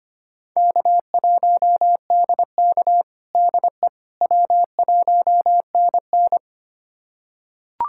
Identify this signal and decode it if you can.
Morse code, heard as a keyed tone, about 25 wpm, 700 Hz. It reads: K1DK DE W1NN
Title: USA Callsign DE Callsign - 4 Characters - 25wpm